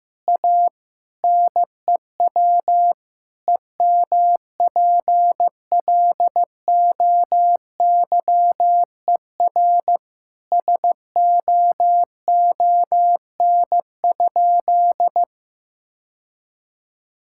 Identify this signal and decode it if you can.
Morse code, heard as a keyed tone, about 15 wpm, 700 Hz. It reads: A NEW EMPLOYER SOON?
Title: Sentences from Top 1600 Words 15wpm